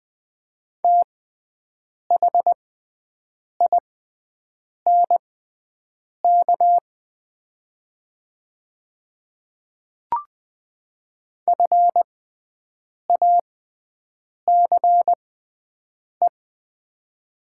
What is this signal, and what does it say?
Morse code, heard as a keyed tone, about 20 wpm, spaced as for 7 wpm, 700 Hz. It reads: THINK FACE
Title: 6x Character Spacing - Top 500 Words - 20wpm